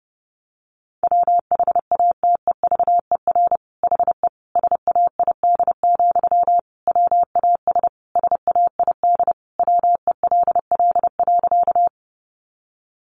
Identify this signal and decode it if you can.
Morse code, heard as a keyed tone, about 30 wpm, 700 Hz. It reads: WHATEVER HE SAID, WAS SAID WELL.
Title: Sentences from Top 1100 Words 30wpm